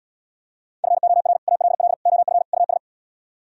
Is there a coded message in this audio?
Yes: Morse code from 0.8 to 2.8 s, heard as a keyed tone, about 80 wpm, 690 Hz. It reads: FOR ALL OF US